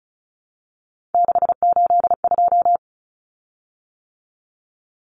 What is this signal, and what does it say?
Morse code, heard as a keyed tone, about 34 wpm, 700 Hz. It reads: 682
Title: Common Cut Numbers - 3-Digit 35wpm